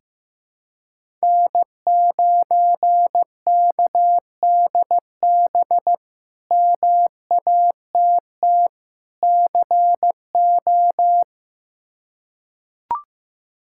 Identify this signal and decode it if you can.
Morse code, heard as a keyed tone, about 16 wpm, 700 Hz. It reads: N9KDB MATT CO
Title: K1USN SST QSO Practice - 15wpm